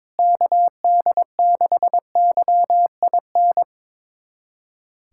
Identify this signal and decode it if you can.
Morse code, heard as a keyed tone, about 22 wpm, 700 Hz. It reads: KD6YIN
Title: USA Callsign DE Callsign 22wpm